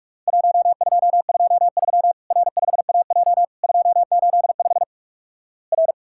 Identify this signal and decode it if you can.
Morse code, heard as a keyed tone, about 46 wpm, 700 Hz. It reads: 1223 R5AJ 285 R
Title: Worked All Europe DX Contest - QTC Exchange - 45wpm